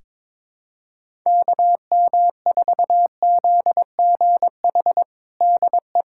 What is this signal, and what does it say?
Morse code, heard as a keyed tone, about 22 wpm, 700 Hz. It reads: KM4ZGH DE
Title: USA Callsign DE Callsign 22wpm